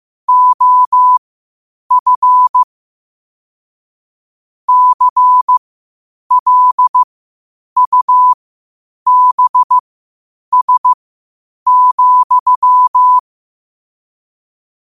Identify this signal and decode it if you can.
Morse code, heard as a keyed tone, about 15 wpm, spaced as for 8 wpm, 1000 Hz. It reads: OF CLUBS,
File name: qotdfiles / 8wpm